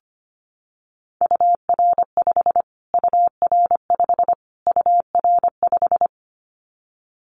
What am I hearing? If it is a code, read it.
Morse code, heard as a keyed tone, about 25 wpm, 700 Hz. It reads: UR5 UR5 UR5
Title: Callsign Prefixes for Top 25 DXCC - Repeated 3x - 25wpm